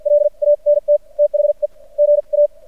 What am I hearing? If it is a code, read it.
Morse code, heard as a keyed tone, about 52 wpm, spaced as for 41 wpm, 590 Hz. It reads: LANT THE GA
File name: archived / CRSnet